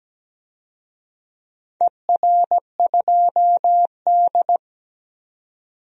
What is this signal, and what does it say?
Morse code, heard as a keyed tone, about 17 wpm, 700 Hz. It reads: ER2D
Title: DXCC-100 Call Signs - 4 Characters - 17wpm